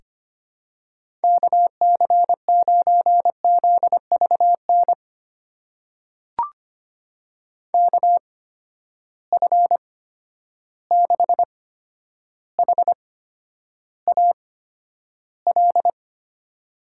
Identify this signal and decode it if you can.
Morse code, heard as a keyed tone, about 25 wpm, spaced as for 11 wpm, 700 Hz. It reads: KC9ZVN KF6HAL
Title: USA Call Signs - 6 Characters - 8x Character Spacing - 25wpm